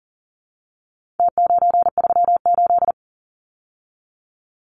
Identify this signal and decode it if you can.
Morse code, heard as a keyed tone, about 40 wpm, 700 Hz. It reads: T938
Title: Numbers - Contest Style - 4-Digit 40wpm